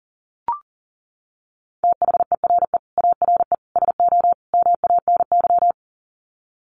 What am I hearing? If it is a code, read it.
Morse code, heard as a keyed tone, about 40 wpm, 700 Hz. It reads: THERE ARE SO MANY
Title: Sets of 4 Words 40wpm